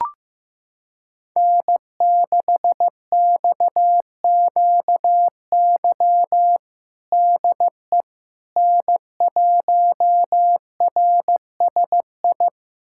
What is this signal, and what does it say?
Morse code, heard as a keyed tone, about 15 wpm, 700 Hz. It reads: N6XQY DE N1RSI
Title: USA Callsign DE Callsign - 5 Characters - 15wpm